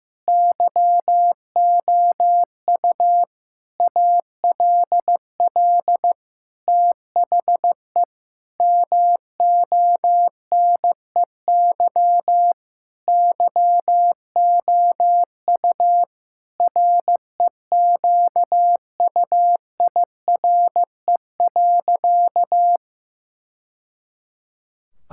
Morse code, heard as a keyed tone, about 15 wpm, 700 Hz. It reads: YOU ALL THE MONEY YOU REQUIRE.
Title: Sentences from Top 1000 Words - Review Part 6 - 15wpm